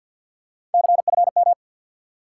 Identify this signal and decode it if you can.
Morse code, heard as a keyed tone, about 53 wpm, 700 Hz. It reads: CFM
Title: Top 90 QSO Elements 50wpm